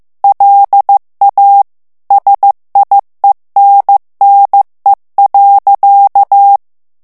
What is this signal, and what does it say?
Morse code, heard as a keyed tone, about 15 wpm, 800 Hz. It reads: LA SIENNE.
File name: FRANCAIS TEXTE CLAIR 2 VITESSE 14 MOTS MINUTES